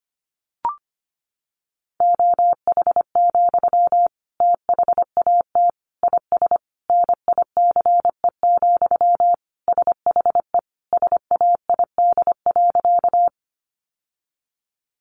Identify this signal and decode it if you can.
Morse code, heard as a keyed tone, about 25 wpm, 700 Hz. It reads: OH, THAT IS NICE, SHE SAID.
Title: Sentences from Top 1100 Words 25wpm